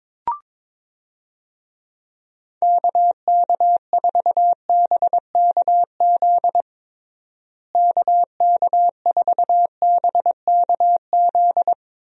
Morse code, heard as a keyed tone, about 22 wpm, 700 Hz. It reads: KK4BKZ KK4BKZ
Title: USA Call Signs - 6 Characters - Repeated 3x - 22wpm